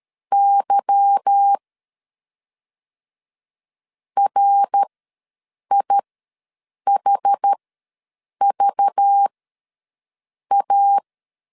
Morse code, heard as a keyed tone, about 13 wpm, spaced as for 7 wpm, 800 Hz. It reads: Y RIHVA
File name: LEZIONI / 8 YJ45 / 8 Y ws 12wpm 3409